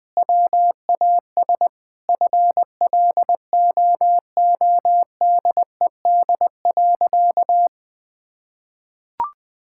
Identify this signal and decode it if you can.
Morse code, heard as a keyed tone, about 20 wpm, 700 Hz. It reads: WAS FLOODED.
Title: Sentences with 6-Word Phrases - 20wpm